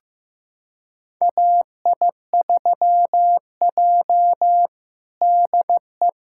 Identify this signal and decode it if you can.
Morse code, heard as a keyed tone, about 15 wpm, 700 Hz. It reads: AI3J DE
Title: USA Callsign DE Callsign - 4 Characters - 15wpm